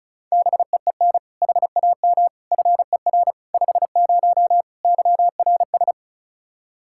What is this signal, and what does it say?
Morse code, heard as a keyed tone, about 34 wpm, 700 Hz. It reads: BEEN HAM FER 50 YRS
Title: Practice QSO - Exchange 3 - More Info - 35wpm